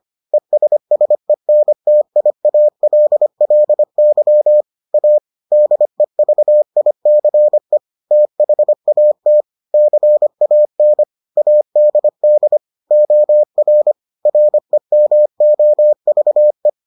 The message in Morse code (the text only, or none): ESSENTIALLY A DEVICE THAT CAN ADD OR REMOVE